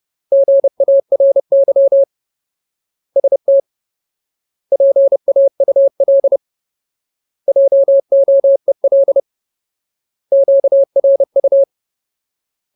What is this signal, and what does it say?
Morse code, heard as a keyed tone, about 30 wpm, 550 Hz. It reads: GARY ST PAUL JOEL QRU
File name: Ultimate Word List / 30WPM Ultimate